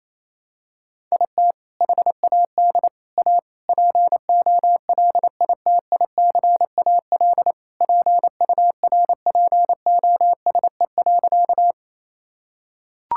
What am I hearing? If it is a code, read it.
Morse code, heard as a keyed tone, about 27 wpm, 700 Hz. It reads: IT HAD A POLITICAL PURPOSE.
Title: Sentences from Top 1000 Words - Review Part 5 - 28wpm